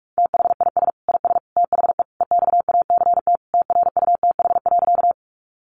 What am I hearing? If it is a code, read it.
Morse code, heard as a keyed tone, about 44 wpm, 700 Hz. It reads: THIS IS THE EXACT TRUTH.
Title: Sentences from Top 1300 Words 45wpm